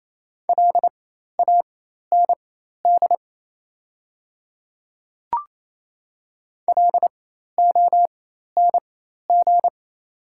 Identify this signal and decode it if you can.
Morse code, heard as a keyed tone, about 28 wpm, spaced as for 13 wpm, 700 Hz. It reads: LAND LONG
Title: 4x Character Spacing - Top 500 Words - 28wpm